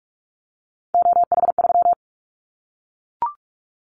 Morse code, heard as a keyed tone, about 44 wpm, 700 Hz. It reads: OH3
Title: Callsign Prefixes for Top 50 DXCC - 45wpm